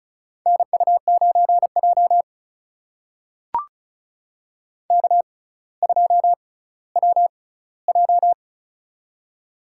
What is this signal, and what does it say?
Morse code, heard as a keyed tone, about 36 wpm, spaced as for 18 wpm, 700 Hz. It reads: NU9J K2WJ
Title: USA Call Signs - 4 Characters - 6x Character Spacing - 35wpm